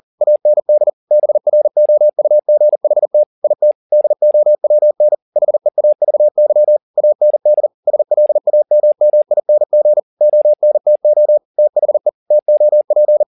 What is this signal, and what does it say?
Morse code, heard as a keyed tone, about 40 wpm, 600 Hz. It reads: AND BROUGHT IT DOWN HEAVY AND SLAMMING ONTO THE TOP
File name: the old man and the sea 2 40wpm